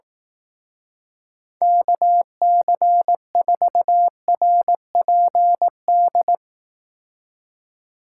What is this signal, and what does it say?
Morse code, heard as a keyed tone, about 18 wpm, 700 Hz. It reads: KC4RPD
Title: Speed Racing - USA 2x3 Call Signs 15wpm